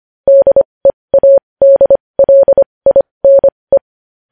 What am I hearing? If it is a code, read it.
Morse code, heard as a keyed tone, about 25 wpm, spaced as for 20 wpm, 550 Hz. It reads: DEADLINE
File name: podcast / News headlines-1x-25x20